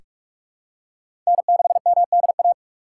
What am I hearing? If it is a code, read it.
Morse code, heard as a keyed tone, about 46 wpm, 700 Hz. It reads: N6MDA
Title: USA Call Signs - 2x Character Spacing - 45wpm